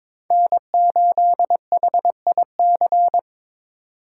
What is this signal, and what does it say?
Morse code, heard as a keyed tone, about 21 wpm, 700 Hz. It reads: N8HIC
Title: USA Call Signs - 5 Characters - 4x Character Spacing - 22wpm